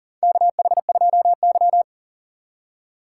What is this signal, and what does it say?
Morse code, heard as a keyed tone, about 44 wpm, 700 Hz. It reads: KH2Y
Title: USA Call Signs - 4 Characters - 4x Character Spacing - 40wpm